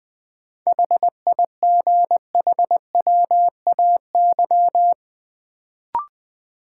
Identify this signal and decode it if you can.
Morse code, heard as a keyed tone, about 20 wpm, 700 Hz. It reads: HIGHWAY
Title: Sentences from Top 1400 Words - New Word Drill - 20wpm